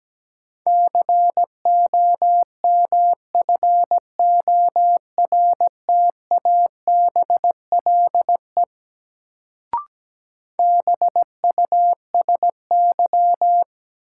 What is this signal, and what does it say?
Morse code, heard as a keyed tone, about 17 wpm, 700 Hz. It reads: COMFORTABLE BUSY
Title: Sentences from Top 1500 Words - New Word Drill Alt - 17wpm